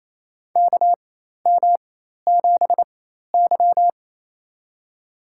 Morse code, heard as a keyed tone, about 28 wpm, spaced as for 13 wpm, 700 Hz. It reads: KM7Y
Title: USA Call Signs - 4x Character Spacing - 28wpm